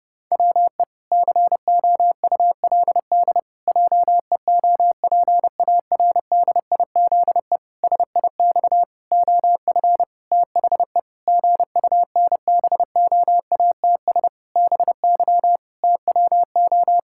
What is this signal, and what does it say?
Morse code, heard as a keyed tone, about 30 wpm, 700 Hz. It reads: WE COULD JEOPARDIZE SIX OF THE GUNBOATS BY TWO
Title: Pangram Challenge v2 - 30wpm